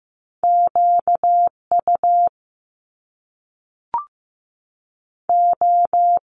Morse code, heard as a keyed tone, about 15 wpm, 700 Hz. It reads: Q U O
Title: Top Two Letter Combinations in Words 15wpm